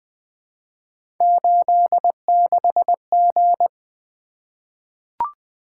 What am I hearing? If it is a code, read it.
Morse code, heard as a keyed tone, about 20 wpm, 700 Hz. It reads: 86G
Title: Three Letter-Number Combinations - 20wpm